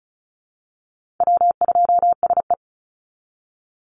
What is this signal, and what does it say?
Morse code, heard as a keyed tone, about 34 wpm, 700 Hz. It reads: W2SE
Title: NA SOTA Chaser Callsigns 35wpm